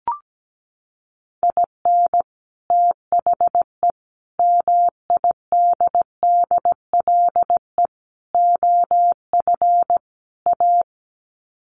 Morse code, heard as a keyed tone, about 17 wpm, 700 Hz. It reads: IN THE MIDDLE OF A
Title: Sets of 5 Words 17wpm